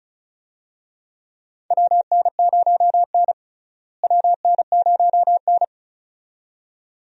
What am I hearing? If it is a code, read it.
Morse code, heard as a keyed tone, about 36 wpm, 700 Hz. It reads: WN0N WN0N
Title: USA Call Signs - 4 Characters - Repeated 2x - 35wpm